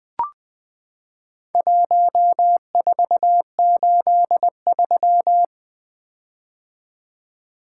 Morse code, heard as a keyed tone, about 20 wpm, 700 Hz. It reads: 1483
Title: Four Letter-Number Combinations - 20wpm